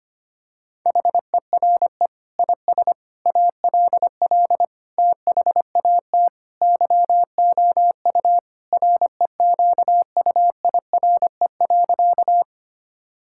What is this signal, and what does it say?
Morse code, heard as a keyed tone, about 25 wpm, 700 Hz. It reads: HERE IS ALL THAT YOU REQUIRE.
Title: Sentences from Top 1000 Words - Review Part 6 - 25wpm